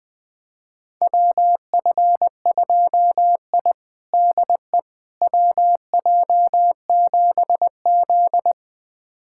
Morse code, heard as a keyed tone, about 20 wpm, 700 Hz. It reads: WF2I DE WJ7Z
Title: USA Callsign DE Callsign - 4 Characters - 20wpm